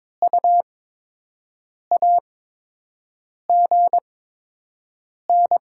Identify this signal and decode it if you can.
Morse code, heard as a keyed tone, about 22 wpm, spaced as for 6 wpm, 700 Hz. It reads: U AGN
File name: Top 20 QSO Elements - 8x Character Spacing - 22wpm